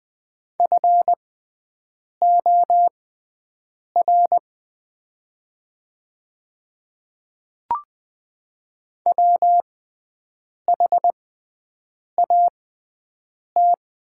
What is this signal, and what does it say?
Morse code, heard as a keyed tone, about 20 wpm, spaced as for 7 wpm, 700 Hz. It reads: FOR WHAT
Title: 6x Character Spacing - Top 200 Words - 20wpm